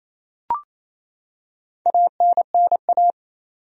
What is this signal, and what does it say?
Morse code, heard as a keyed tone, about 28 wpm, 700 Hz. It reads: ANNA
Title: USA Common Names - Max 4 Letters - 28wpm